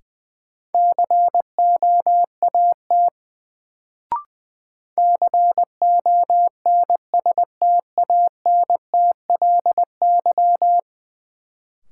Morse code, heard as a keyed tone, about 21 wpm, 700 Hz. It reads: COAT CONSTANTLY
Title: Sentences from Top 1600 Words - New Word Drill Alt - 20wpm